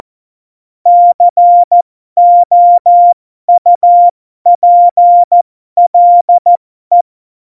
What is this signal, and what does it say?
Morse code, heard as a keyed tone, about 14 wpm, spaced as for 12 wpm, 700 Hz. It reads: COUPLE
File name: Morsekurs / Witze